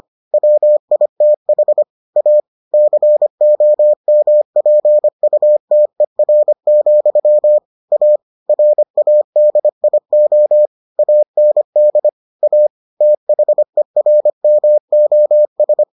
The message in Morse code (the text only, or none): WITH A COMPUTER, A RADIO AND A THERMOS